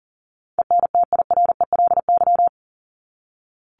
Morse code, heard as a keyed tone, about 40 wpm, 700 Hz. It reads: ENTIRELY
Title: LY Suffix - Top 100 Words with Base Words 40wpm